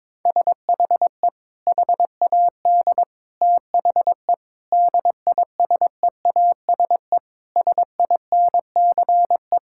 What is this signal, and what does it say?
Morse code, heard as a keyed tone, about 22 wpm, 700 Hz. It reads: SHE HAD THE DISEASE SINCE
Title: Sentences from Top 1000 Words - Review Part 6 - 22wpm